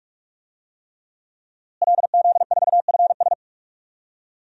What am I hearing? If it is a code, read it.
Morse code, heard as a keyed tone, about 43 wpm, 700 Hz. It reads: LZ4FS
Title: DXCC-100 Call Signs - 5 Characters - Repeated 3x - 45wpm